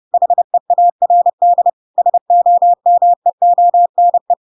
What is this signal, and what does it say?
Morse code, heard as a keyed tone, about 30 wpm, 700 Hz. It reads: HEARD SOMEONE
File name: Pinocchio / Pinocchio ch4 30 30 ik1hge